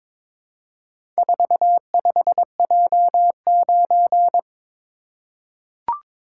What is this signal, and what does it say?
Morse code, heard as a keyed tone, about 22 wpm, 700 Hz. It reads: 45J9
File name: practice / 22wpm / Four Letter-Number Combinations - 22wpm